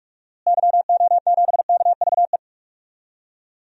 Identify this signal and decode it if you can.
Morse code, heard as a keyed tone, about 46 wpm, 700 Hz. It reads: YO7KVE
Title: DXCC-100 Call Signs - 6 Characters - Repeated 3x - 45wpm